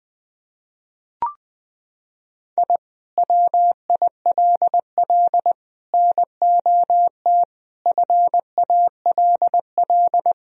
Morse code, heard as a keyed tone, about 20 wpm, 700 Hz. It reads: I WILL NOT FALL
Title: Sentences from Top 1000 Words - Review Part 2 - 20wpm